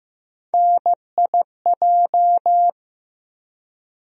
Morse code, heard as a keyed tone, about 15 wpm, 700 Hz. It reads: NIJ
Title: USA Call Sign Postfixes - 3 Letters 15wpm